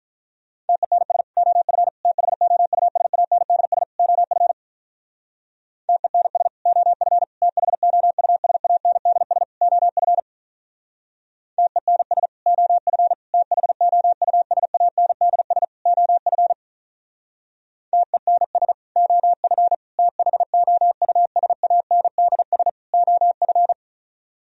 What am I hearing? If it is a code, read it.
Morse code, heard as a keyed tone, about 46 wpm, 700 Hz. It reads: TENS OF THOUSANDS OF TENS OF THOUSANDS OF TENS OF THOUSANDS OF TENS OF THOUSANDS OF